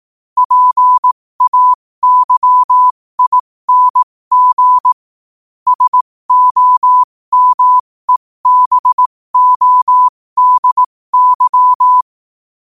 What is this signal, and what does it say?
Morse code, heard as a keyed tone, about 18 wpm, spaced as for 15 wpm, 1000 Hz. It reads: PAYING SOMEBODY
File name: qotdfiles / 15wpm